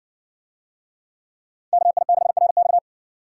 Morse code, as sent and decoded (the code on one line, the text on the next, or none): -.- .. -.... .-. -..-
KI6RX